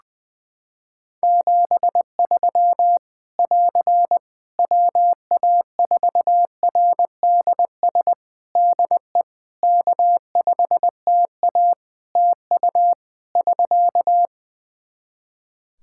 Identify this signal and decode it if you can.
Morse code, heard as a keyed tone, about 21 wpm, 700 Hz. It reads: 73 <AR> WA4RDS DE K5TA TU <SK>
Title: Practice QSO - Exchange 4 - Ending - 20wpm